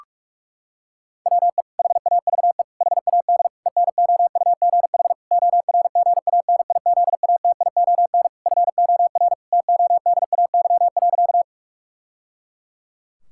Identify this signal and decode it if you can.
Morse code, heard as a keyed tone, about 46 wpm, 700 Hz. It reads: WE HAVE HAD ENOUGH ORGANIZATION FOR TODAY.
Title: Sentences from Top 1000 Words - Review Part 5 - 45wpm